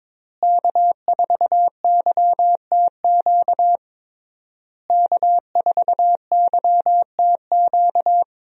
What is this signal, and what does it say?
Morse code, heard as a keyed tone, about 22 wpm, 700 Hz. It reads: K4YTQ K4YTQ
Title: USA Call Signs - 5 Characters - Repeated 3x - 22wpm